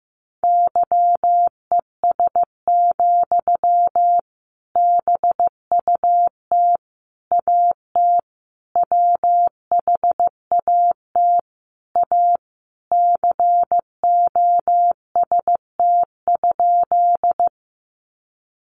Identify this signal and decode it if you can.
Morse code, heard as a keyed tone, about 15 wpm, 700 Hz. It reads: YES, BUT AT WHAT A COST?
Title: Sentences from Top 800 Words 15wpm